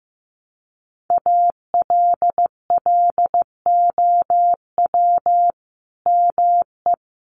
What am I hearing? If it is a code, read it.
Morse code, heard as a keyed tone, about 15 wpm, 700 Hz. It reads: ALLOW ME
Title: Sentences from Top 1100 Words 15wpm